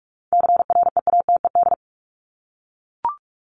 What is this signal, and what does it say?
Morse code, heard as a keyed tone, about 44 wpm, 700 Hz. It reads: CREATED
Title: ED Suffix - Top 100 Words with Base Words 45wpm